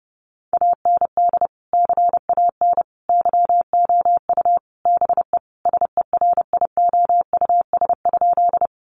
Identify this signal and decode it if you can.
Morse code, heard as a keyed tone, about 30 wpm, 700 Hz. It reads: AND CAN YOU BE SERIOUS?
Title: Sentences from Top 1100 Words 30wpm